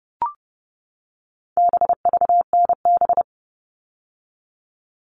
Morse code, heard as a keyed tone, about 30 wpm, 700 Hz. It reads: BVNB
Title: All Cut Numbers - 4-Digit 30wpm